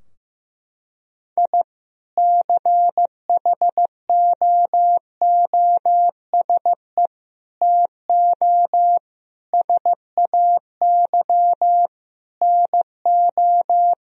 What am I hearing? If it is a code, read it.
Morse code, heard as a keyed tone, about 15 wpm, 700 Hz. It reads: I CHOOSE TO SAY NO
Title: Sentences from Top 1000 Words - Review Part 4 - 15wpm